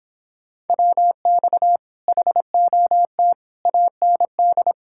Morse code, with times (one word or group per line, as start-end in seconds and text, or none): 0.7-1.8 WX
2.1-3.3 HOT
3.7-4.7 AND